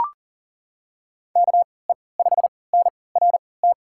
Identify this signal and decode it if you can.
Morse code, heard as a keyed tone, about 40 wpm, spaced as for 23 wpm, 700 Hz. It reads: KE5NRT
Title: USA Call Signs - 6 Characters - 3x Character Spacing - 40wpm